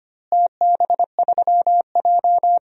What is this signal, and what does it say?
Morse code, heard as a keyed tone, about 25 wpm, 700 Hz. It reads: TB3J
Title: DXCC-100 Call Signs - 4 Characters - Repeated 2x - 25wpm